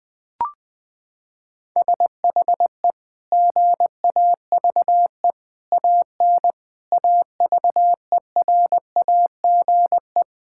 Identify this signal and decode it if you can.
Morse code, heard as a keyed tone, about 19 wpm, 700 Hz. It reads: SHE GAVE AN AVERAGE
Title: Sentences from Top 1000 Words - Review Part 6 - 20wpm